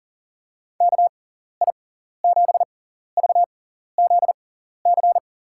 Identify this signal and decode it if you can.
Morse code, heard as a keyed tone, about 38 wpm, spaced as for 14 wpm, 700 Hz. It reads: KI7VZC
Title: USA Call Signs - 6 Characters - 6x Character Spacing - 40wpm